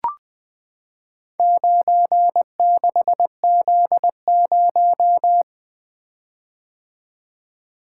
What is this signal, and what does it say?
Morse code, heard as a keyed tone, about 21 wpm, 700 Hz. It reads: 96Z0
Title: Four Letter-Number Combinations - 20wpm